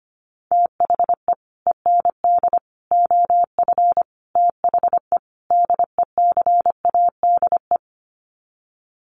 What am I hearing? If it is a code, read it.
Morse code, heard as a keyed tone, about 25 wpm, 700 Hz. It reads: THE END OF THE DECADE
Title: Sets of 5 Words Encore 25wpm